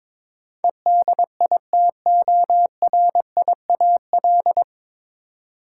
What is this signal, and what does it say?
Morse code, heard as a keyed tone, about 22 wpm, 700 Hz. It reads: EDITORIAL
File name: practice / 22wpm / Nine Letter Words - 22wpm